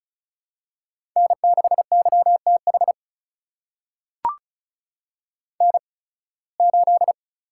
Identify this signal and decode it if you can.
Morse code, heard as a keyed tone, about 34 wpm, spaced as for 23 wpm, 700 Hz. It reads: N6YTH N8
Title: USA Call Signs - 5 Characters - 8x Character Spacing - 35wpm